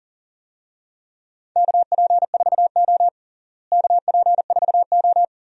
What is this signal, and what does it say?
Morse code, heard as a keyed tone, about 37 wpm, 700 Hz. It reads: KP4O KP4O
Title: DXCC-100 Call Signs - 4 Characters - Repeated 3x - 40wpm